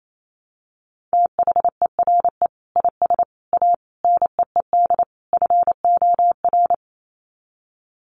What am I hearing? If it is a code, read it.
Morse code, heard as a keyed tone, about 28 wpm, 700 Hz. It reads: THERE IS A NEED FOR
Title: Sets of 5 Words Encore2 28wpm